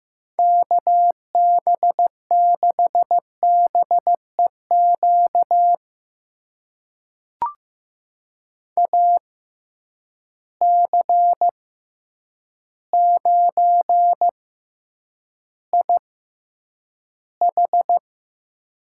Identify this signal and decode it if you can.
Morse code, heard as a keyed tone, about 15 wpm, spaced as for 8 wpm, 700 Hz. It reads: KB6BEQ AC9IH
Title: USA Call Signs - 6x Character Spacing - 15wpm